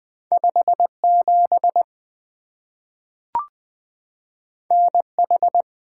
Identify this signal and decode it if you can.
Morse code, heard as a keyed tone, about 20 wpm, 700 Hz. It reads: 5 7 N H
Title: Two Letter-Number Combinations - 20wpm